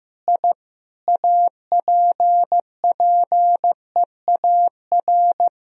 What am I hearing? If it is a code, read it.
Morse code, heard as a keyed tone, about 15 wpm, 700 Hz. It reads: I APPEAR
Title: Sentences from Top 1000 Words - Review Part 2 - 15wpm